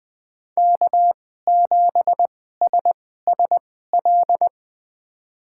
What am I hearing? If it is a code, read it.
Morse code, heard as a keyed tone, about 20 wpm, spaced as for 14 wpm, 700 Hz. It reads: K7SSL
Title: USA Call Signs - 2x Character Spacing - 20wpm